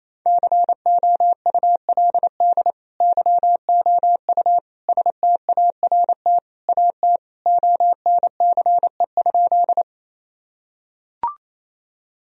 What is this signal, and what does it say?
Morse code, heard as a keyed tone, about 28 wpm, 700 Hz. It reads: COULD YOU START AT ONCE?
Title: Sentences from Top 1000 Words - Review Part 1 - 28wpm